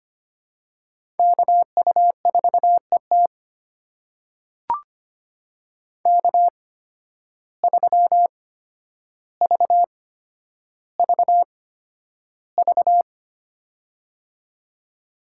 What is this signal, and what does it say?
Morse code, heard as a keyed tone, about 25 wpm, spaced as for 11 wpm, 700 Hz. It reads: KU4ET K3VVV
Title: USA Call Signs - 5 Characters - 8x Character Spacing - 25wpm